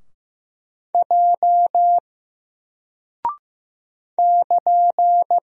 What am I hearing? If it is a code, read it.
Morse code, heard as a keyed tone, about 15 wpm, 700 Hz. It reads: J <KN>
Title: Single Character (Letter:Number:Punctuation:Prosign) Alt 15wpm